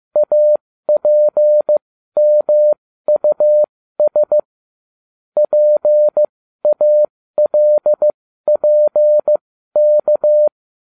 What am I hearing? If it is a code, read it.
Morse code, heard as a keyed tone, about 15 wpm, spaced as for 12 wpm, 600 Hz. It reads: APMUS PALPK